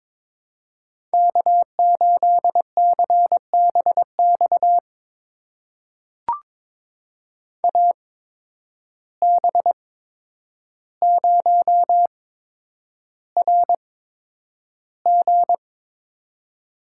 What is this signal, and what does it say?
Morse code, heard as a keyed tone, about 22 wpm, spaced as for 9 wpm, 700 Hz. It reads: K8CBX AB0RG
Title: USA Call Signs - 5 Characters - 8x Character Spacing - 22wpm